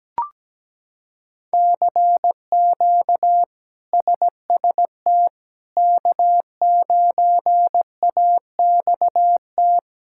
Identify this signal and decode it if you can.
Morse code, heard as a keyed tone, about 17 wpm, 700 Hz. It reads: CQ SST K9AXT
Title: K1USN SST QSO Practice - 17wpm